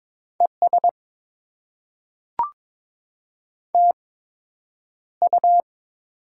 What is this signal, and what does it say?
Morse code, heard as a keyed tone, about 22 wpm, spaced as for 9 wpm, 700 Hz. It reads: ES TU
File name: Top 20 QSO Elements - 8x Character Spacing - 22wpm